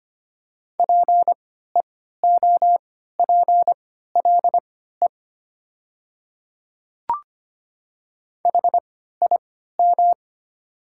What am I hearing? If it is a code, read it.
Morse code, heard as a keyed tone, about 25 wpm, spaced as for 14 wpm, 700 Hz. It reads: PEOPLE HIM
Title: 3x Character Spacing - Top 200 Words - 25wpm